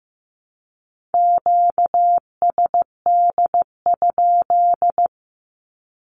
Morse code, heard as a keyed tone, about 15 wpm, 700 Hz. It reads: QSD?
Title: Q-Codes Repeated 3x 15wpm